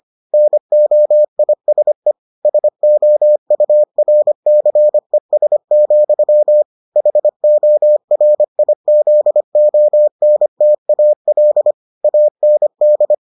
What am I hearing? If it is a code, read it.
Morse code, heard as a keyed tone, about 25 wpm, 600 Hz. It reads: NOISE SOURCES, HORIZONTAL AND